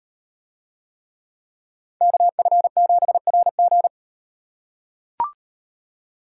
Morse code, heard as a keyed tone, about 35 wpm, 700 Hz. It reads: KF7RG